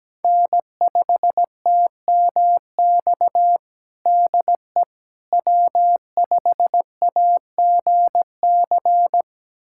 Morse code, heard as a keyed tone, about 17 wpm, 700 Hz. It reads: N5TMX DE W5AGC
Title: USA Callsign DE Callsign - 5 Characters - 17wpm